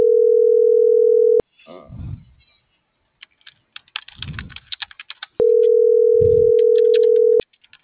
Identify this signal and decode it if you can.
Ringback tone (North America), 440 + 480 Hz.